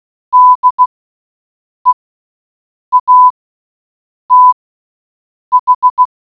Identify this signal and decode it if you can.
Morse code, heard as a keyed tone, about 16 wpm, spaced as for 7 wpm, 1000 Hz. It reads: DEATH